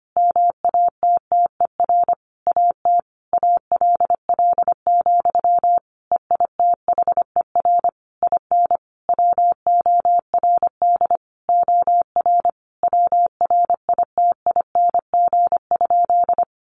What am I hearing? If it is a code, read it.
Morse code, heard as a keyed tone, about 25 wpm, 700 Hz. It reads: MATTER AT ALL, EITHER IN WORD OR WRITING?
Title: Sentences from Top 1200 Words 25wpm